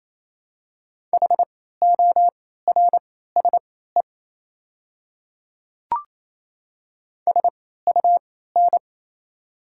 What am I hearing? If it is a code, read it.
Morse code, heard as a keyed tone, about 27 wpm, spaced as for 16 wpm, 700 Hz. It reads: HORSE SUN
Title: 3x Character Spacing - Top 300 Words - 28wpm